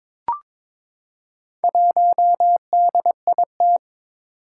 Morse code, heard as a keyed tone, about 23 wpm, 700 Hz. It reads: 1DIT
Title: Four Challenging Letter-Number Combinations 22wpm